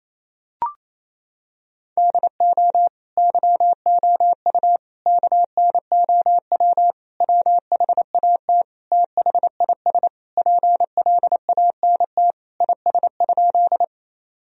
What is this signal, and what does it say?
Morse code, heard as a keyed tone, about 28 wpm, 700 Hz. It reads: DO YOU KNOW WHAT THIS PLANT IS?
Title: Sentences from Top 1000 Words - Review Part 1 - 28wpm